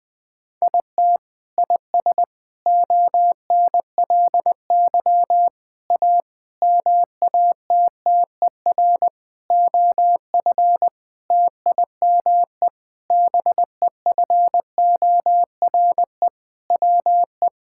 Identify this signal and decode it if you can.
Morse code, heard as a keyed tone, about 20 wpm, 700 Hz. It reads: IT IS ONLY A MATTER OF TIME BEFORE WE
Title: Sentences with 8-Word Phrases - Spoken Beginning - 20wpm